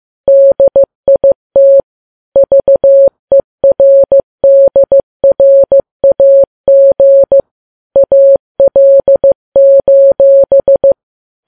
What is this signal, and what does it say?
Morse code, heard as a keyed tone, about 15 wpm, 550 Hz. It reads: DIT VERDRAG AL:
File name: Files Web tls / Morse on9fv / morse15kps